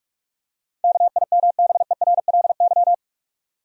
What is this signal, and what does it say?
Morse code, heard as a keyed tone, about 46 wpm, 700 Hz. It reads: KIMBERLY
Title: USA Common Names - 45wpm